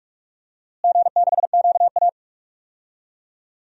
Morse code, heard as a keyed tone, about 47 wpm, 700 Hz. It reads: G6QA
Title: DXCC-100 Call Signs - 4 Characters - Repeated 2x - 45wpm